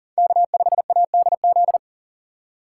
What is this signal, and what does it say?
Morse code, heard as a keyed tone, about 42 wpm, 700 Hz. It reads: K5ADZ
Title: USA Call Signs - 5 Characters - 6x Character Spacing - 40wpm